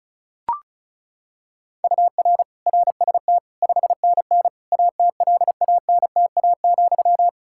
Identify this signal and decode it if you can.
Morse code, heard as a keyed tone, about 36 wpm, 700 Hz. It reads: UR RST 5NN ATLANTA,
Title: Practice QSO - Exchange 1 - Beginning - 35wpm